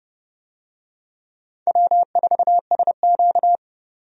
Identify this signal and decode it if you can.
Morse code, heard as a keyed tone, about 30 wpm, 700 Hz. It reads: W4SQ
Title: USA Call Signs - 4 Characters - 30wpm